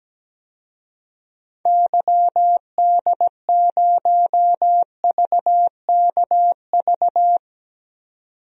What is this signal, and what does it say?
Morse code, heard as a keyed tone, about 17 wpm, 700 Hz. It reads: YD0VKV
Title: DXCC-100 Call Signs - 6 Characters - 17wpm